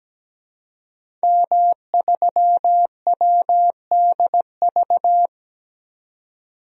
Morse code, heard as a keyed tone, about 17 wpm, 700 Hz. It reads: M3WDV
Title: DXCC-100 Call Signs - 5 Characters - 17wpm